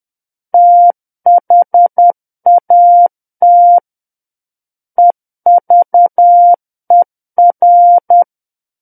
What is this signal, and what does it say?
Morse code, heard as a keyed tone, about 10 wpm, 700 Hz. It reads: THAT EVER
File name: Book One / 01-Chapter 1 - 10 WPM